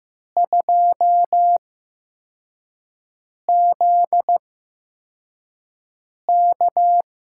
Morse code, heard as a keyed tone, about 15 wpm, spaced as for 4 wpm, 700 Hz. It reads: 2ZK
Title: USA Call Signs - 4 Characters - 8x Character Spacing - 15wpm